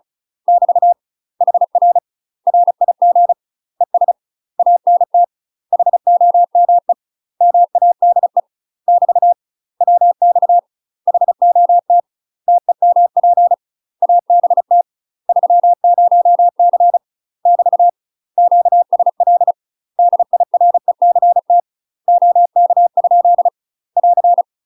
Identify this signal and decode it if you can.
Morse code, heard as a keyed tone, about 36 wpm, 700 Hz. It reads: <BT> HR RIG ES ANT HOME MADE <BT> WX HOT TEMP ABT 30C <BT> QSL DIRECT OK? <AR>
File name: UK MASTER QSOs TRAINING / 35WPM / UK MASTER QSOs 0036